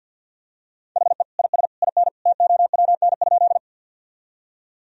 Morse code, heard as a keyed tone, about 52 wpm, 700 Hz. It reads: HE IS IN TOWN?
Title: Sentences from Top 1000 Words - Review Part 2 - 50wpm